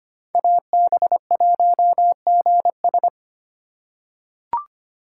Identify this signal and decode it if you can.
Morse code, heard as a keyed tone, about 25 wpm, 700 Hz. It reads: AB1GS
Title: USA Call Signs - 2x Character Spacing - 25wpm